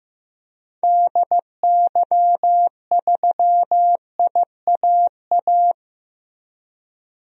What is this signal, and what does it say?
Morse code, heard as a keyed tone, about 15 wpm, 700 Hz. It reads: DY3IAA
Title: DXCC-100 Call Signs - 6 Characters - Repeated 3x - 15wpm